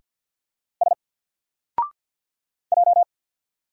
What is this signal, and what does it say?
Morse code, heard as a keyed tone, about 49 wpm, 700 Hz. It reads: S J
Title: Single Letter-Number 50wpm